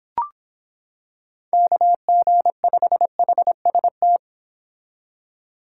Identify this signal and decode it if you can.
Morse code, heard as a keyed tone, about 26 wpm, 700 Hz. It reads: KG5HST